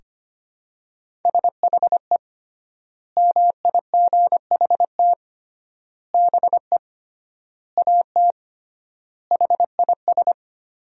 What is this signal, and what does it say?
Morse code, heard as a keyed tone, about 25 wpm, 700 Hz. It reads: SHE MIGHT BE AT HIS
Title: 3x Word-Spacing - Sentences from Top 300 Words - 25wpm